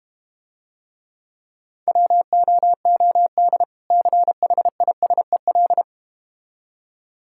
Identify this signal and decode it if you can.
Morse code, heard as a keyed tone, about 31 wpm, 700 Hz. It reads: WOOD CHISEL
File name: practice / 25wpm / Difficult Words - Speed-Racing with Hints - 25wpm